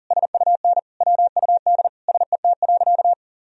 Morse code, heard as a keyed tone, about 40 wpm, 700 Hz. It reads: SUN WUD SET.
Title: Sentences from Top 400 Words and Abbreviations 40wpm